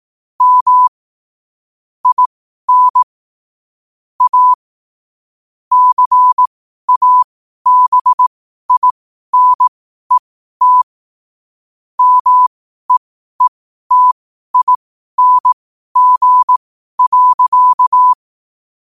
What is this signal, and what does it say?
Morse code, heard as a keyed tone, about 18 wpm, spaced as for 12 wpm, 1000 Hz. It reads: M IN A CABINET MEETING.
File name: qotdfiles / 12wpm